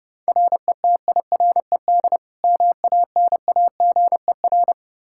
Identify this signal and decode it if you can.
Morse code, heard as a keyed tone, about 30 wpm, 700 Hz. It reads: RETIRED MANAGER
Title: K1USN SST QSO Practice 30wpm